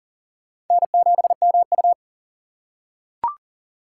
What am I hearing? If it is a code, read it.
Morse code, heard as a keyed tone, about 40 wpm, 700 Hz. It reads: N7MU
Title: USA Call Signs - 4 Characters - 6x Character Spacing - 40wpm